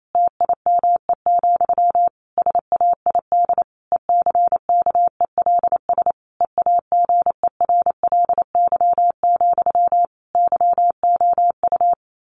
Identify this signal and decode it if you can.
Morse code, heard as a keyed tone, about 28 wpm, 700 Hz. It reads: TIME, SAID ECKELS EAGERLY, YOU
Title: A Sound Of Thunder 28wpm